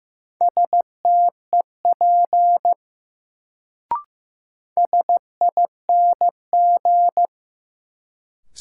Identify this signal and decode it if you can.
Morse code, heard as a keyed tone, about 15 wpm, 700 Hz. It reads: STEP SING
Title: Sentences from Top 400 Words - New Word Drill - 15wpm